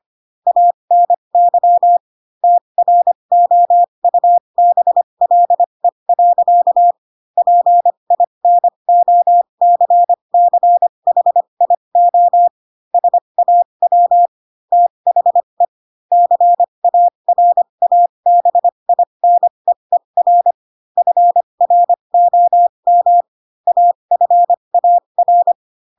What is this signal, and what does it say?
Morse code, heard as a keyed tone, about 25 wpm, spaced as for 22 wpm, 700 Hz. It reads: ANY TROUBLE. PINOCCHIO SAW THE CARABINEER FROM AFAR
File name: Pinocchio / Pinocchio ch3 18 25 ik1hge